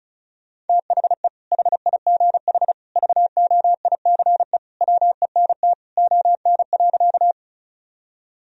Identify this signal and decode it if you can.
Morse code, heard as a keyed tone, about 34 wpm, 700 Hz. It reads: THE HIGH VOICE WENT ON.
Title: Sentences from Top 1000 Words - Review Part 1 - 35wpm